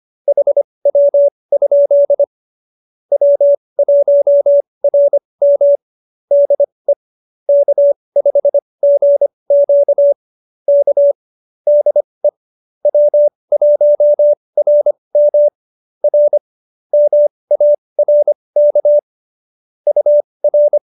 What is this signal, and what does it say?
Morse code, heard as a keyed tone, about 25 wpm, spaced as for 20 wpm, 570 Hz. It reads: HW? W1RM DE K5GQ K DE W1RM R MARK UR